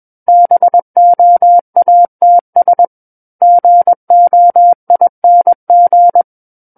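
Morse code, heard as a keyed tone, about 21 wpm, 700 Hz. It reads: BOATS GOING
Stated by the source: Morse code practice reading of a book chapter at 21 words per minute.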